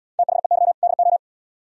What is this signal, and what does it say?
Morse code, heard as a keyed tone, about 75 wpm, 700 Hz. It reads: THEY DO